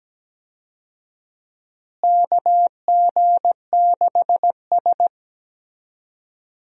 Morse code, heard as a keyed tone, about 17 wpm, 700 Hz. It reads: KG6S
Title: USA Call Signs - 4 Characters - Repeated 3x - 17wpm